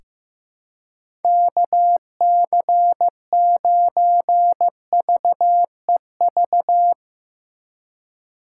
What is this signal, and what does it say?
Morse code, heard as a keyed tone, about 15 wpm, 700 Hz. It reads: KC9VEV
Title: USA Call Signs - 6 Characters - 4x Character Spacing - 15wpm